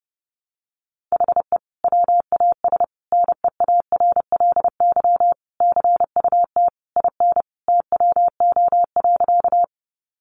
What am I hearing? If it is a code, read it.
Morse code, heard as a keyed tone, about 30 wpm, 700 Hz. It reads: HE WAS NEARLY CUT IN TWO.
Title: Sentences from Top 1300 Words 30wpm